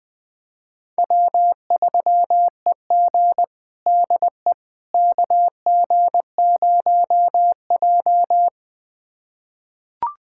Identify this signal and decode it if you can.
Morse code, heard as a keyed tone, about 20 wpm, 700 Hz. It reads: W3EG DE KG0J
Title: USA Callsign DE Callsign - 4 Characters - 20wpm